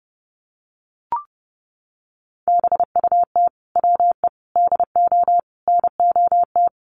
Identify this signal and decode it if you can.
Morse code, heard as a keyed tone, about 30 wpm, 700 Hz. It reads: BUT WE DO NOT
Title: Sets of 5 Words Encore2 30wpm